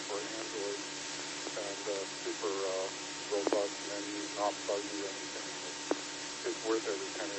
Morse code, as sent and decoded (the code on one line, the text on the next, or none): none